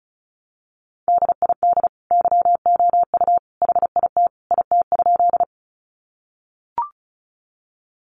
Morse code, heard as a keyed tone, about 34 wpm, 700 Hz. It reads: DID YOU HIT IT?
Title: Sentences from Top 1200 Words 35wpm